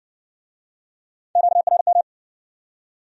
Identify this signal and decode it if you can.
Morse code, heard as a keyed tone, about 57 wpm, 700 Hz. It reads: QRM